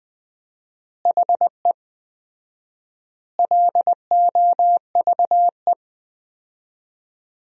Morse code, heard as a keyed tone, about 20 wpm, 700 Hz. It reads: HE LOVE
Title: 4x Word-Spacing - Sentences from Top 400 Words - 20wpm